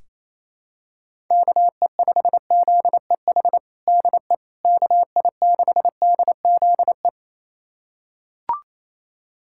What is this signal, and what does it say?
Morse code, heard as a keyed tone, about 27 wpm, 700 Hz. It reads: KE5ZEH DE KI6DZE
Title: USA Callsign DE Callsign 28wpm